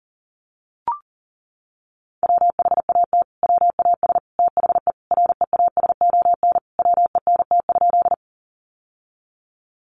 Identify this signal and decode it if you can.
Morse code, heard as a keyed tone, about 40 wpm, 700 Hz. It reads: WHAT WAS THE REASON WENT?
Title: Sentences from Top 700 Words 40wpm